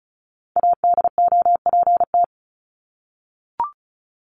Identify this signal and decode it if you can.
Morse code, heard as a keyed tone, about 34 wpm, 700 Hz. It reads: ADOPT
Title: ED Suffix - Top 100 Words with Base Words 35wpm